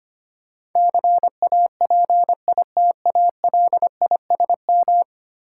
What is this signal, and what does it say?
Morse code, heard as a keyed tone, about 25 wpm, 700 Hz. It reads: CAPITALISM
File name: Ten Letter Words - 25wpm